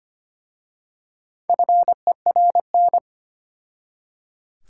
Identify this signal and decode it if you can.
Morse code, heard as a keyed tone, about 25 wpm, 700 Hz. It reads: FERN
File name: practice / 25wpm / Difficult Words with Hints - 25wpm